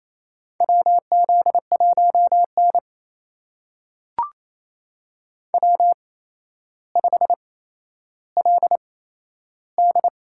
Morse code, heard as a keyed tone, about 28 wpm, spaced as for 12 wpm, 700 Hz. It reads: WZ1N W5LD
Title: USA Call Signs - 4 Characters - 8x Character Spacing - 28wpm